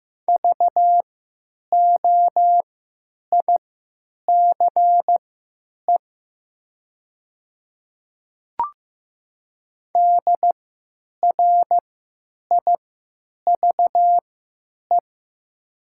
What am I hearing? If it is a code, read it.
Morse code, heard as a keyed tone, about 15 wpm, spaced as for 9 wpm, 700 Hz. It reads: VOICE DRIVE
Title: 3x Character Spacing - Top 500 Words - 15wpm